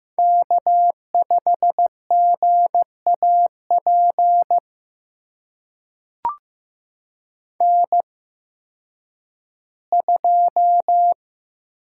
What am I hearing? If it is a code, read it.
Morse code, heard as a keyed tone, about 15 wpm, spaced as for 10 wpm, 700 Hz. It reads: K5GAP N2
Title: USA Call Signs - 5 Characters - 8x Character Spacing - 15wpm